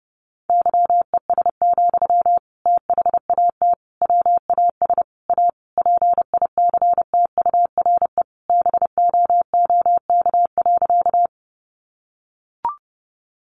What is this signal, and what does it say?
Morse code, heard as a keyed tone, about 28 wpm, 700 Hz. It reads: YES, THAT WAS A PICTURE BOOK.
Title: Sentences from Top 500 Words 30wpm